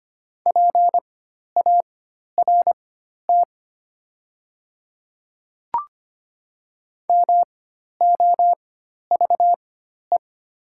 Morse code, heard as a keyed tone, about 25 wpm, spaced as for 12 wpm, 700 Hz. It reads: PART MOVE
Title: 4x Character Spacing - Top 200 Words - 25wpm